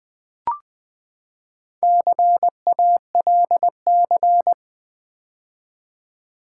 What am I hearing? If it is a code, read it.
Morse code, heard as a keyed tone, about 19 wpm, 700 Hz. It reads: CALC
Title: Build A Word - 4 Letters At A Time - 11-Letter Words - 20wpm